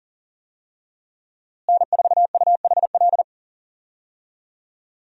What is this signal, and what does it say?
Morse code, heard as a keyed tone, about 37 wpm, 700 Hz. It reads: N4UHL
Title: USA Call Signs - 5 Characters - Repeated 3x - 40wpm